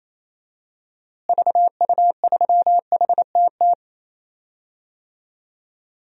Morse code, heard as a keyed tone, about 28 wpm, 700 Hz. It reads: VU3HTT
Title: DXCC-100 Call Signs - 6 Characters - Repeated 2x - 28wpm